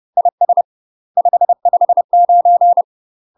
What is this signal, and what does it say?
Morse code, heard as a keyed tone, about 30 wpm, 700 Hz. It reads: IS 559
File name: UK MASTER QSOs TRAINING / 30WPM / UK MASTER QSOs 0004